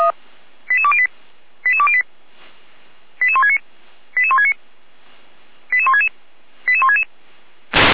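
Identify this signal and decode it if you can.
Five-tone selective call (ZVEI), heard as quick runs of five tones, tones of about 70 ms. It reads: 88208 88208 88169 88169 88166 88166
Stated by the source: Five-tone ZVEI selective call received by SDR, from feuerwehr-orlamuende.org fire brigade audio archive.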